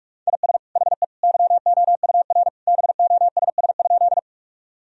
Morse code, heard as a keyed tone, about 46 wpm, 700 Hz. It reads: IS HE YOUR BOSS?
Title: Sentences from Top 1000 Words - Review Part 4 - 45wpm